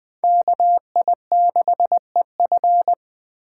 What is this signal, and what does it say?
Morse code, heard as a keyed tone, about 24 wpm, 700 Hz. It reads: KI6EF
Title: Speed Racing - USA 2x2 Call Signs 20wpm